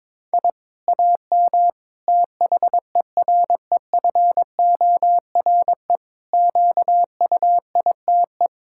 Morse code, heard as a keyed tone, about 22 wpm, 700 Hz. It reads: I AM THEREFORE QUITE
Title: Sentences from Top 1000 Words - Review Part 6 - 22wpm